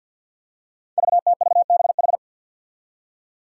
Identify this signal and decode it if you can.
Morse code, heard as a keyed tone, about 50 wpm, 700 Hz. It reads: VTVBH
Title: Five Challenging Letter-Number Combinations 50wpm